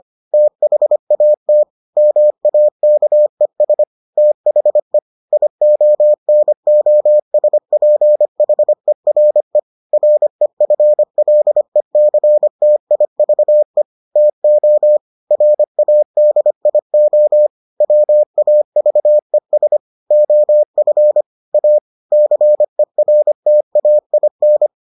Morse code, heard as a keyed tone, about 25 wpm, 600 Hz. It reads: THAT MAKES THE IONOSPHERE REFLECTIVE TO RADIO WAVES OF A CERTAIN